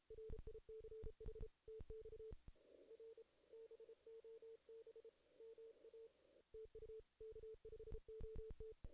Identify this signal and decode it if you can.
Morse code, heard as a keyed tone, about 32 wpm, 440 Hz. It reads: RICH TX R BOB MA TU K5OT
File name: 2016 JAN CW NAQP / 2016 Jan naqp k5ot - 0-8